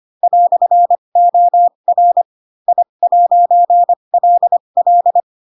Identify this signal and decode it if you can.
Morse code, heard as a keyed tone, about 25 wpm, spaced as for 22 wpm, 700 Hz. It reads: "OR I'LL
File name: Pinocchio / Pinocchio ch7 18 25 ik1hge